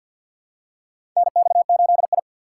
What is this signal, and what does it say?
Morse code, heard as a keyed tone, about 51 wpm, 700 Hz. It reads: NX8I
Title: USA Call Signs - 4 Characters - 3x Character Spacing - 50wpm